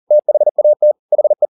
Morse code, heard as a keyed tone, about 40 wpm, 600 Hz. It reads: THAT HE
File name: the old man and the sea 2 40wpm